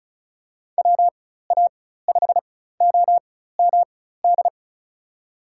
Morse code, heard as a keyed tone, about 36 wpm, spaced as for 16 wpm, 700 Hz. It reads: WA5OMD
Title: USA Call Signs - 6 Characters - 4x Character Spacing - 35wpm